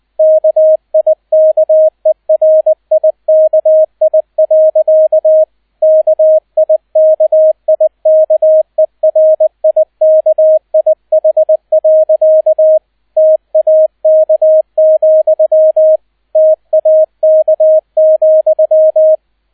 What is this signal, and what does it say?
Morse code, heard as a keyed tone, about 20 wpm, 630 Hz, with radio noise. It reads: KIKERIKI. KIKIKERIKIH. TAK, TAK,
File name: Erster Streich 20wpm